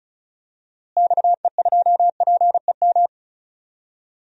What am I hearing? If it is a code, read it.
Morse code, heard as a keyed tone, about 35 wpm, 700 Hz. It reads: XE2PEM
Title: DXCC-100 Call Signs - 6 Characters - Repeated 3x - 35wpm